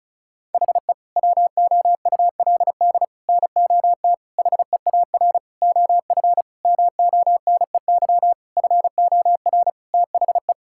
Morse code, heard as a keyed tone, about 34 wpm, 690 Hz. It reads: HE WOULD NOT HEAR OF MONEY FOR THE
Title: Sentences from Top 1000 Words - Review Part 5 - 35wpm